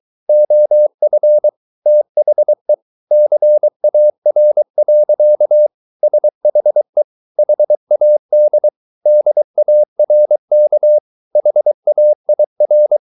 Morse code, heard as a keyed tone, about 23 wpm, 600 Hz. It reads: OF THE CAR. SHE HAD DARK HAIR
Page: Romanen Never af Ken Follet på engelsk.